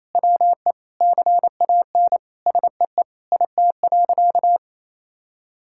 Morse code, heard as a keyed tone, about 28 wpm, 700 Hz. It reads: WE CAN SEE IT.
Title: Sentences from Top 1600 Words 28wpm